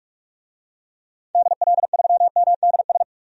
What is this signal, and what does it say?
Morse code, heard as a keyed tone, about 46 wpm, 700 Hz. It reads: DL3MDS
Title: DXCC-100 Call Signs - 6 Characters - 45wpm